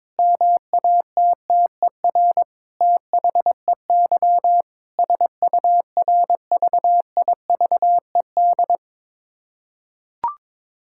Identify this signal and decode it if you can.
Morse code, heard as a keyed tone, about 21 wpm, 700 Hz. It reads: MATTER THEY SURVIVED
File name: Sentences with 7-Word Phrases - 22wpm